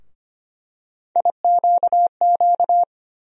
Morse code, heard as a keyed tone, about 25 wpm, 700 Hz. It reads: IQQ
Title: USA Call Sign Postfixes - 3 Letters 25wpm